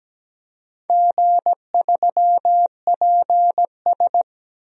Morse code, heard as a keyed tone, about 17 wpm, 700 Hz. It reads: G3PS
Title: DXCC-100 Call Signs - 4 Characters - 17wpm